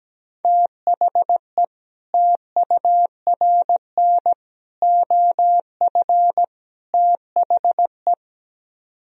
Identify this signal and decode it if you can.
Morse code, heard as a keyed tone, about 17 wpm, 700 Hz. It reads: THE TURN OF THE
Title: Speed-Racing - Sets of 4 Words 17wpm